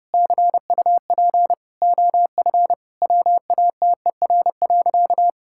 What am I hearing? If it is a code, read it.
Morse code, heard as a keyed tone, about 30 wpm, 700 Hz. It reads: CUP OF WATER.
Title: Sentences from Top 1600 Words 30wpm